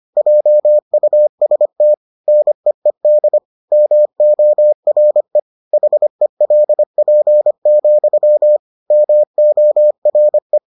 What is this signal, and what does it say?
Morse code, heard as a keyed tone, about 25 wpm, 600 Hz. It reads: JUST NEED MORE HELP, MORE